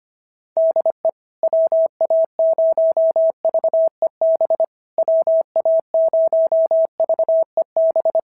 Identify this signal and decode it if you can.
Morse code, heard as a keyed tone, about 24 wpm, 650 Hz. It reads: DE WA0VEB WA0VEB
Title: SOTA QSO Practice 25wpm